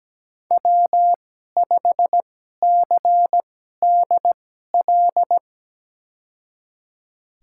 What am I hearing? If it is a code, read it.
Morse code, heard as a keyed tone, about 17 wpm, spaced as for 12 wpm, 700 Hz. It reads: W5CDL
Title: USA Call Signs - 5 Characters - 2x Character Spacing - 17wpm